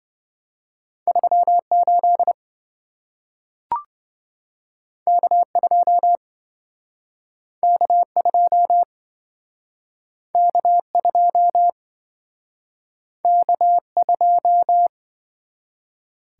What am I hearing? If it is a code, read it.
Morse code, heard as a keyed tone, about 26 wpm, 700 Hz. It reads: 38 K2 K2 K2 K2